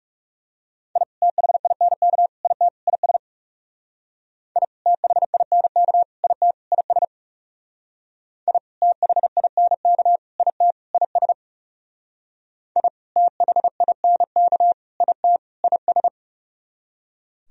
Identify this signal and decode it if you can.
Morse code, heard as a keyed tone, about 36 wpm, 700 Hz. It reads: I THINK IT IS I THINK IT IS I THINK IT IS I THINK IT IS
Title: Speed-Racing - Sets of 4 Words 30wpm